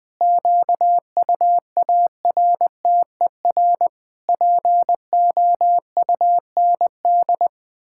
Morse code, heard as a keyed tone, about 20 wpm, 700 Hz. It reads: QUARTER POUND
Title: Pangram Challenge v2 - 20wpm